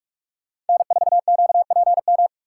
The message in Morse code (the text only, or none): N4QPM